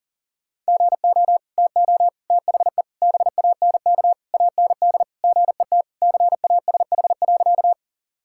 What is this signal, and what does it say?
Morse code, heard as a keyed tone, about 40 wpm, 700 Hz. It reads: GO TO THE BANK AND GET CASH.
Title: Sentences from Top 1000 Words - Review Part 5 - 40wpm